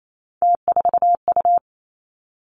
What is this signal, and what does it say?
Morse code, heard as a keyed tone, about 28 wpm, 700 Hz. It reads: T4U
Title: Three Challenging Letter-Number Combinations 28wpm